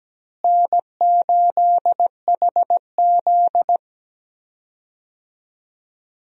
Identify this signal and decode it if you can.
Morse code, heard as a keyed tone, about 17 wpm, 700 Hz. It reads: N8HZ
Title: USA Call Signs - 4 Characters - Repeated 3x - 17wpm